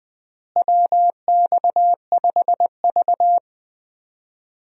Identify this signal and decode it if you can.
Morse code, heard as a keyed tone, about 20 wpm, 700 Hz. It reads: WX5V
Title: USA Call Signs - 4 Characters - Repeated 2x - 20wpm